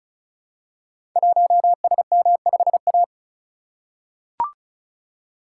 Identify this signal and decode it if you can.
Morse code, heard as a keyed tone, about 36 wpm, 700 Hz. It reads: 1SM5A
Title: Code Groups - 5 Letter:Numbers - 35wpm